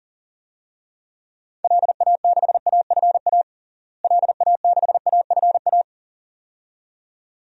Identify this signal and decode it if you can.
Morse code, heard as a keyed tone, about 40 wpm, 700 Hz. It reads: LA6AFA LA6AFA
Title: DXCC-100 Call Signs - 6 Characters - Repeated 2x - 40wpm